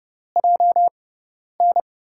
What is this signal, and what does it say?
Morse code, heard as a keyed tone, about 30 wpm, 700 Hz. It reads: J N
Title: USA Call Signs - 6 Characters - 6x Character Spacing - 30wpm